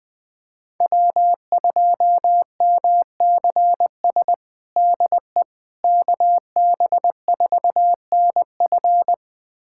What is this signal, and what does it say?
Morse code, heard as a keyed tone, about 20 wpm, 700 Hz. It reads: W2MCS DE KB4NF
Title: USA Callsign DE Callsign - 5 Characters - 20wpm